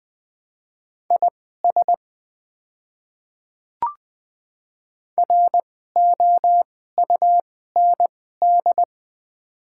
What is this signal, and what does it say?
Morse code, heard as a keyed tone, about 20 wpm, spaced as for 14 wpm, 700 Hz. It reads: IS ROUND
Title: 2x Character Spacing - Top 300 Words - 20wpm